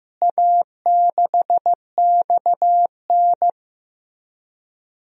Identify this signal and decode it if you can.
Morse code, heard as a keyed tone, about 15 wpm, 700 Hz. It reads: A6XN
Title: DXCC-100 Call Signs - 4 Characters - Repeated 3x - 15wpm